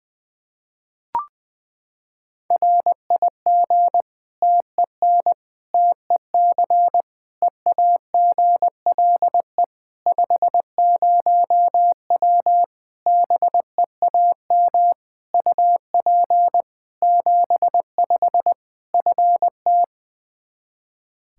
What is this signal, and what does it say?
Morse code, heard as a keyed tone, about 21 wpm, 700 Hz. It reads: RIG TEN TEC EAGLE 50W BEAM UP 75 FT
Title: Practice QSO - Exchange 2 - Info - 20wpm